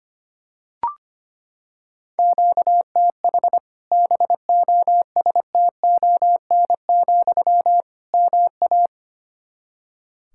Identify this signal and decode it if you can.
Morse code, heard as a keyed tone, about 25 wpm, 700 Hz. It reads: QTH BOSTON, MA
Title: Practice QSO - Exchange 1 - Beginning - 25wpm